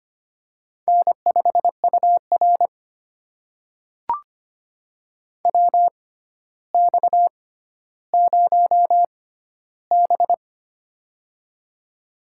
Morse code, heard as a keyed tone, about 25 wpm, spaced as for 13 wpm, 700 Hz. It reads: N5UR WX0B
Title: USA Call Signs - 4 Characters - 6x Character Spacing - 25wpm